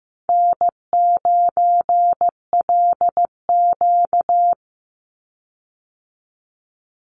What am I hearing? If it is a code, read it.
Morse code, heard as a keyed tone, about 15 wpm, 700 Hz. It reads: N9LQ
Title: Call Signs Repeated 2x - 15wpm